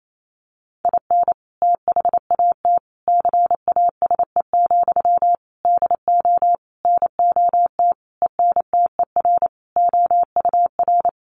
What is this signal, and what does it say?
Morse code, heard as a keyed tone, about 28 wpm, 700 Hz. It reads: IN THAT CASE, DO NOT ENTER OUR
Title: Sentences from Top 1300 Words 28wpm